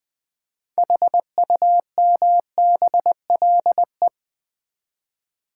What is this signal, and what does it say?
Morse code, heard as a keyed tone, about 20 wpm, 700 Hz. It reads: HUMBLE
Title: 6 Letter Words - First 3 Letters Given - 20wpm